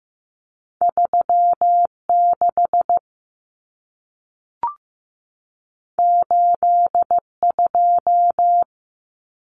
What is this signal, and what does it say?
Morse code, heard as a keyed tone, about 15 wpm, 700 Hz. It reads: 36 82
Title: Common Cut Numbers - 2-Digit 15wpm